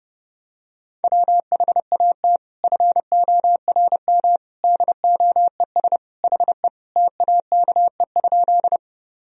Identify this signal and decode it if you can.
Morse code, heard as a keyed tone, about 30 wpm, 700 Hz. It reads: WHAT FORM DOES HE TAKE?
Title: Sentences from Top 1000 Words - Review Part 1 - 30wpm